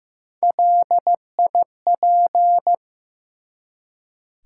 Morse code, heard as a keyed tone, about 15 wpm, 700 Hz. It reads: LIP
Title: Three Letter Words - 15wpm